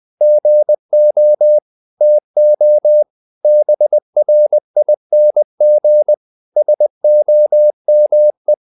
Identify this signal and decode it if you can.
Morse code, heard as a keyed tone, about 20 wpm, 600 Hz. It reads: GO TO BRING SOME
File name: the old man and the sea part 2 20wpm